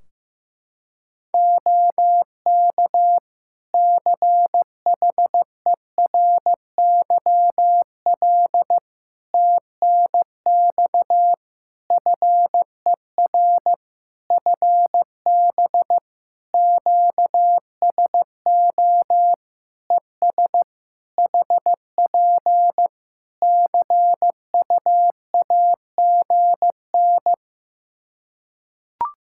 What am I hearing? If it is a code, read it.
Morse code, heard as a keyed tone, about 15 wpm, 700 Hz. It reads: OK CHERYL TNX FER FB QSO ES HP CUAGN
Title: Practice QSO - Exchange 4 - Ending - 15wpm